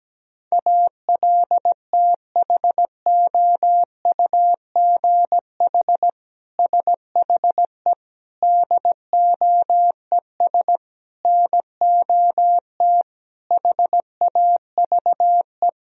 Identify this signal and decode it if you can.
Morse code, heard as a keyed tone, about 17 wpm, 700 Hz. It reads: ALTHOUGH SHE DOES NOT HAVE
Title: Sentences from Top 1000 Words - Review Part 3 - 17wpm